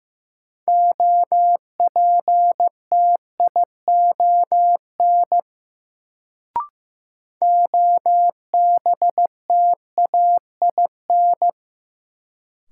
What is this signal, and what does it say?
Morse code, heard as a keyed tone, about 16 wpm, 700 Hz. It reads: OPTION OBTAIN
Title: Sentences from Top 1300 Words - New Word Drill Alt - 15wpm